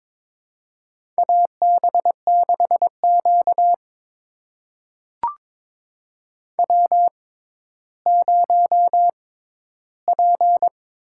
Morse code, heard as a keyed tone, about 22 wpm, spaced as for 12 wpm, 700 Hz. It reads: AB6Q W0P
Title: USA Call Signs - 4 Characters - 6x Character Spacing - 22wpm